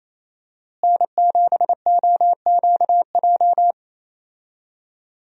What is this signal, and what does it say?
Morse code, heard as a keyed tone, about 28 wpm, 700 Hz. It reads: N7OQJ
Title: USA Call Signs - 5 Characters - Repeated 3x - 28wpm